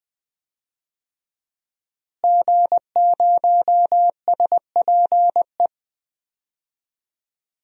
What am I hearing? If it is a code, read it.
Morse code, heard as a keyed tone, about 20 wpm, 700 Hz. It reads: G0SPE
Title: DXCC-100 Call Signs - 5 Characters - Repeated 3x - 20wpm